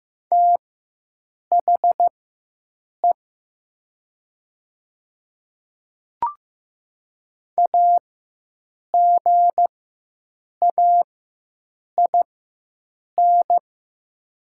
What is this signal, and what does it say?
Morse code, heard as a keyed tone, about 15 wpm, spaced as for 7 wpm, 700 Hz. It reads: THE AGAIN
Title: 4x Character Spacing - Top 200 Words - 15wpm